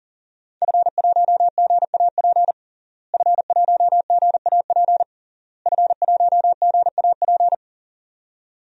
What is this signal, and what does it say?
Morse code, heard as a keyed tone, about 40 wpm, 700 Hz. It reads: F1GAP F1GAP F1GAP
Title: DXCC-100 Call Signs - 5 Characters - Repeated 3x - 40wpm